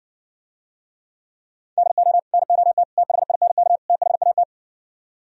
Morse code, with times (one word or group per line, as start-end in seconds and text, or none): 1.8-2.2 DO
2.3-2.8 NOT
3.0-3.8 THINK
3.9-4.4 THAT